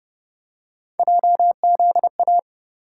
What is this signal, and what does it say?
Morse code, heard as a keyed tone, about 31 wpm, 700 Hz. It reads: JZA